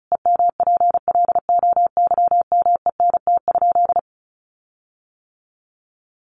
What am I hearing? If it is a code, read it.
Morse code, heard as a keyed tone, about 34 wpm, 700 Hz. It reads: EMPLOYMENT?
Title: Sentences from Top 1500 Words 35wpm